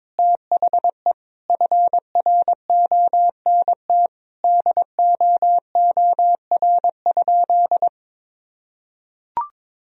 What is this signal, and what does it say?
Morse code, heard as a keyed tone, about 22 wpm, 700 Hz. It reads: THE FRONT DOOR?
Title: Sentences from Top 1000 Words - Review Part 2 - 22wpm